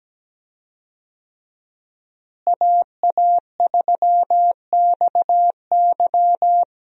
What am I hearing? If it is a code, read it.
Morse code, heard as a keyed tone, about 17 wpm, 700 Hz. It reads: AA3XY
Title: USA Call Signs - 5 Characters - Repeated 2x - 17wpm